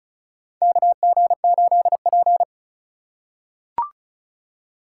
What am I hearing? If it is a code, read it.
Morse code, heard as a keyed tone, about 34 wpm, 700 Hz. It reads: KG8P
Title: USA Call Signs - 4 Characters - 3x Character Spacing - 35wpm